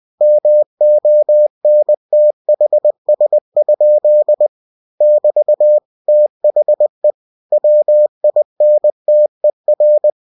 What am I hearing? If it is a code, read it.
Morse code, heard as a keyed tone, about 20 wpm, 600 Hz. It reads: MONTHS? <BT> THE WINTER
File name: cw-books / winnie-the-pooh / winnie-20wpm-page-0065